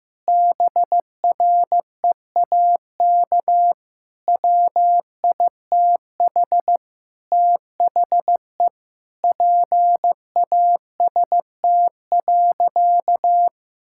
Morse code, heard as a keyed tone, about 15 wpm, 700 Hz. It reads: BREAK WITH THE PAST.
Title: Sentences from Top 1000 Words - Review Part 6 - 15wpm